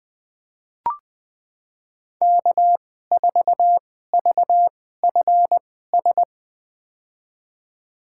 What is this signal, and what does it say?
Morse code, heard as a keyed tone, about 20 wpm, spaced as for 14 wpm, 700 Hz. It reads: K4VFS
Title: USA Call Signs - 5 Characters - 2x Character Spacing - 20wpm